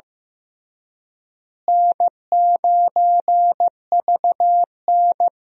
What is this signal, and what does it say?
Morse code, heard as a keyed tone, about 15 wpm, 700 Hz. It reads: N9VN
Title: USA Call Signs - 4 Characters - Repeated 3x - 15wpm